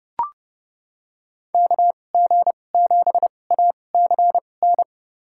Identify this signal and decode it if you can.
Morse code, heard as a keyed tone, about 32 wpm, spaced as for 22 wpm, 700 Hz. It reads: KG7ACN
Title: USA Call Signs - 6 Characters - 2x Character Spacing - 30wpm